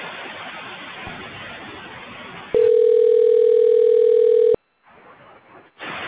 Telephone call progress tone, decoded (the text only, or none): ringback tone